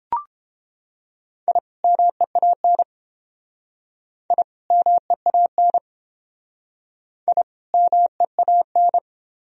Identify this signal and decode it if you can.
Morse code, heard as a keyed tone, about 30 wpm, 700 Hz. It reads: I MEAN I MEAN I MEAN